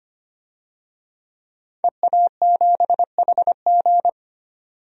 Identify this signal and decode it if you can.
Morse code, heard as a keyed tone, about 25 wpm, 700 Hz. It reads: EA7HG
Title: DXCC-100 Call Signs - 5 Characters - 25wpm